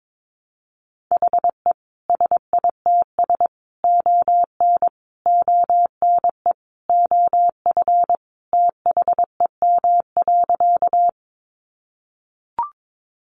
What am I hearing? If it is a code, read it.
Morse code, heard as a keyed tone, about 22 wpm, 700 Hz. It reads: HE SITS ON ONE OF THEM.
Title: Additional Easy Phrases 22wpm